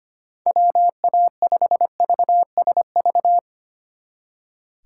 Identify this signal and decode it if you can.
Morse code, heard as a keyed tone, about 32 wpm, 700 Hz. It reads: WA5VSV